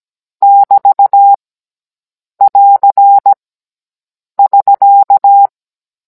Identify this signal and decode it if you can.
Morse code, heard as a keyed tone, about 17 wpm, 800 Hz. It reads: <BT> <AR> <SK>